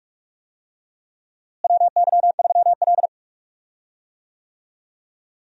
Morse code, heard as a keyed tone, about 42 wpm, 700 Hz. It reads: WY3L
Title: USA Call Signs - 4 Characters - Repeated 3x - 45wpm